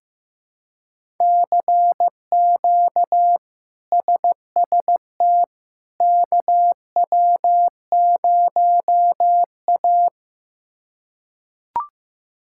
Morse code, heard as a keyed tone, about 15 wpm, 700 Hz. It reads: CQ SST KW0A
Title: K1USN SST QSO Practice - 15wpm